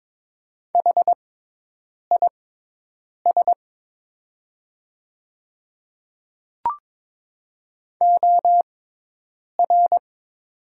Morse code, heard as a keyed tone, about 22 wpm, spaced as for 8 wpm, 700 Hz. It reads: HIS OR
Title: 6x Character Spacing - Top 100 Words - 22wpm